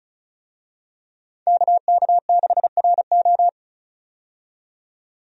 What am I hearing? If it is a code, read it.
Morse code, heard as a keyed tone, about 36 wpm, 700 Hz. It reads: KK6RO
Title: USA Call Signs - 5 Characters - 35wpm